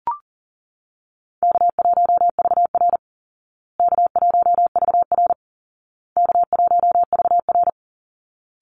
Morse code, heard as a keyed tone, about 40 wpm, 700 Hz. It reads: K1VR K1VR K1VR
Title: Easy Call Signs Repeated 3x - 40wpm